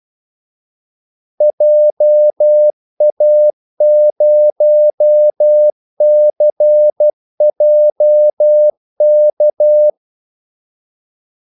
Morse code, heard as a keyed tone, about 12 wpm, 600 Hz. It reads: JA0CJK
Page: 2 ekstra ordmellemrum mellem kaldesignalerne